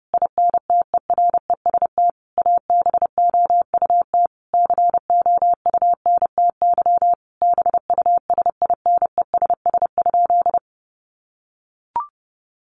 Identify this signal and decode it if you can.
Morse code, heard as a keyed tone, about 28 wpm, 700 Hz. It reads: INTEREST ABOUT COUNTY BUSINESS?
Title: Sentences from Top 1200 Words 30wpm